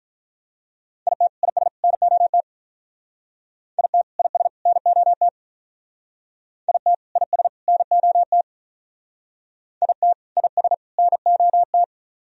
Morse code, heard as a keyed tone, about 43 wpm, 700 Hz. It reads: IT IS NOT IT IS NOT IT IS NOT IT IS NOT
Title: Speed-Racing - Sets of 3 Words 35wpm